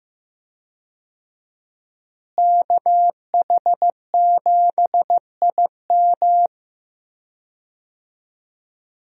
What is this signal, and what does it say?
Morse code, heard as a keyed tone, about 15 wpm, 700 Hz. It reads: KH7IM
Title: USA Call Signs - 5 Characters - Repeated 3x - 15wpm